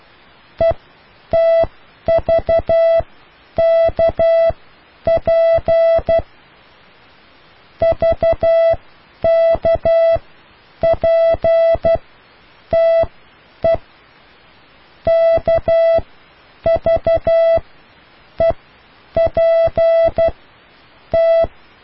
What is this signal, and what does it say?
Morse code, heard as a keyed tone, about 12 wpm, spaced as for 8 wpm, 660 Hz. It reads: ETVKP VKPTE KVEPT